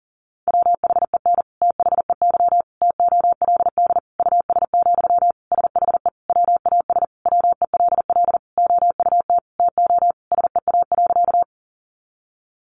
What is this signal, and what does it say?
Morse code, heard as a keyed tone, about 40 wpm, 700 Hz. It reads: WHEN THEY TOLD US, SHE WAS WELL OUT TO SEA.
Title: Sentences from Top 400 Words 40wpm